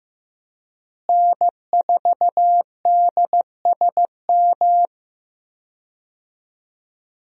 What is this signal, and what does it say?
Morse code, heard as a keyed tone, about 15 wpm, 700 Hz. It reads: N4DSM
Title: USA Call Signs - 5 Characters - Repeated 3x - 15wpm